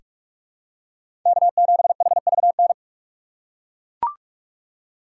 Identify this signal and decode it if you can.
Morse code, heard as a keyed tone, about 41 wpm, 700 Hz. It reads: K7HVN
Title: USA Call Signs - 5 Characters - 4x Character Spacing - 45wpm